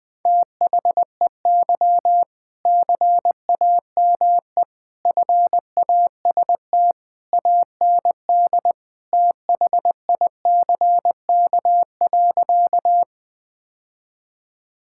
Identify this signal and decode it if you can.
Morse code, heard as a keyed tone, about 20 wpm, 700 Hz. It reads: THEY CAME FAST AND THICK.
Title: Sentences from Top 1600 Words 20wpm